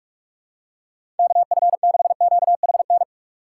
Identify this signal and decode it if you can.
Morse code, heard as a keyed tone, about 46 wpm, 700 Hz. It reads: KF6QHN
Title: USA Call Signs - 6 Characters - 3x Character Spacing - 45wpm